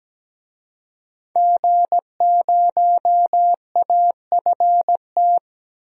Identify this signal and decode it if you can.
Morse code, heard as a keyed tone, about 17 wpm, 700 Hz. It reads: G0AFT
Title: DXCC-100 Call Signs - 5 Characters - 17wpm